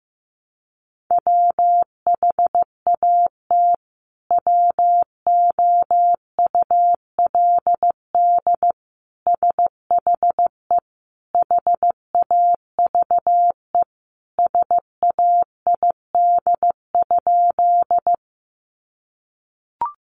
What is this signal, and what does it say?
Morse code, heard as a keyed tone, about 15 wpm, 700 Hz. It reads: WHAT WOULD SHE HAVE SAID?
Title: Sentences from Top 100 Words 15wpm